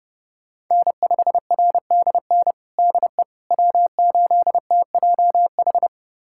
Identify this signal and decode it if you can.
Morse code, heard as a keyed tone, about 32 wpm, 700 Hz. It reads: N5RDN DE W8TJH
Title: USA Callsign DE Callsign - 5 Characters - 30wpm